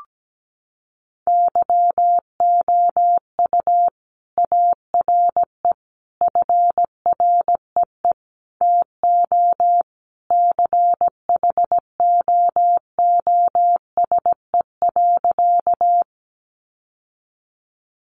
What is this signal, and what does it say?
Morse code, heard as a keyed tone, about 17 wpm, 700 Hz. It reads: YOU ARE FREE TO CHOOSE.
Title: Sentences from Top 700 Words 17wpm